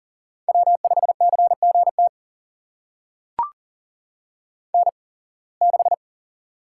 Morse code, heard as a keyed tone, about 40 wpm, spaced as for 26 wpm, 700 Hz. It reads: W5CGT N6
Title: USA Call Signs - 5 Characters - 8x Character Spacing - 40wpm